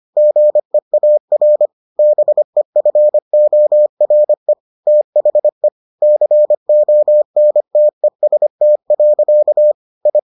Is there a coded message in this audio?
Yes: Morse code from 0.2 to 10.2 s, heard as a keyed tone, about 25 wpm, 600 Hz. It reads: GEAR BEFORE THE CONTEST. I